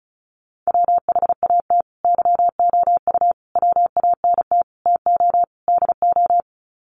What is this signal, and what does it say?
Morse code, heard as a keyed tone, about 34 wpm, 700 Hz. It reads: WHAT YOU WANT TO DO
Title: Sets of 5 Words Encore 35wpm